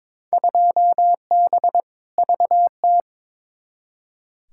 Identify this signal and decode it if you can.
Morse code, heard as a keyed tone, about 22 wpm, 700 Hz. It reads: 2B VT
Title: Field Day Exchanges Encore - 22wpm